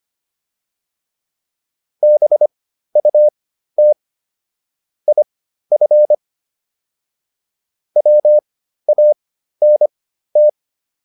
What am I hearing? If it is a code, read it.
Morse code, heard as a keyed tone, about 25 wpm, spaced as for 13 wpm, 600 Hz. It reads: BUT IF WANT